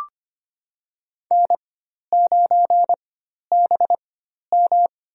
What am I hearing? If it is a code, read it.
Morse code, heard as a keyed tone, about 25 wpm, spaced as for 12 wpm, 700 Hz. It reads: N9BM
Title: USA Call Signs - 4x Character Spacing - 25wpm